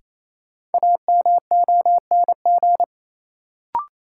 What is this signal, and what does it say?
Morse code, heard as a keyed tone, about 28 wpm, 700 Hz. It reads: AMONG
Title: Sets of 2 Words - New Word Drill - 28wpm